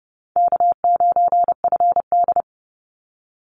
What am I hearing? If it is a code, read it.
Morse code, heard as a keyed tone, about 30 wpm, 700 Hz. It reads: K9FD
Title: Easy Call Signs Repeated 3x - 30wpm